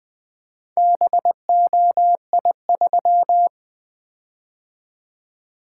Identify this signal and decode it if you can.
Morse code, heard as a keyed tone, about 21 wpm, 700 Hz. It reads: BOI3
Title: Four Letter-Number Combinations - 20wpm